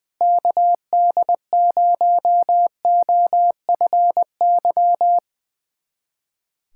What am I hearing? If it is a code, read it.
Morse code, heard as a keyed tone, about 20 wpm, 700 Hz. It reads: KD0OFY
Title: USA Callsign DE Callsign 20wpm